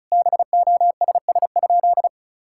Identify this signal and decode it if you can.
Morse code, heard as a keyed tone, about 36 wpm, 700 Hz. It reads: BOSS?
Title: Sentences from Top 1000 Words - Review Part 6 - 35wpm